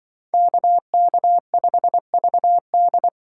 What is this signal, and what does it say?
Morse code, heard as a keyed tone, about 25 wpm, 700 Hz. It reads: KK5VD
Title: Speed Racing - USA 2x2 Call Signs 20wpm